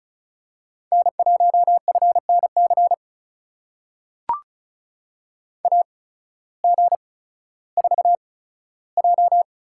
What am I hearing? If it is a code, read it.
Morse code, heard as a keyed tone, about 34 wpm, spaced as for 16 wpm, 690 Hz. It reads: N1FNC AG4J
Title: USA Call Signs - 5 Characters - 8x Character Spacing - 35wpm